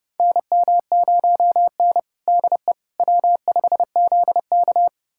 Morse code, heard as a keyed tone, about 32 wpm, 700 Hz. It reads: NM0N DE W5ZK
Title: USA Callsign DE Callsign - 4 Characters - 30wpm